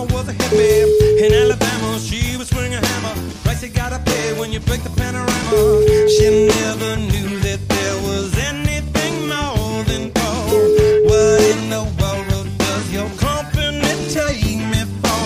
Ringback tone (Europe, 425 Hz), with noise.